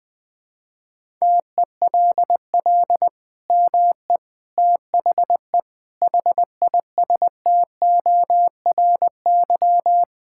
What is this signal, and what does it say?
Morse code, heard as a keyed tone, about 20 wpm, 700 Hz. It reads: TELL ME THE HISTORY
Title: Sentences from Top 1000 Words - Review Part 5 - 20wpm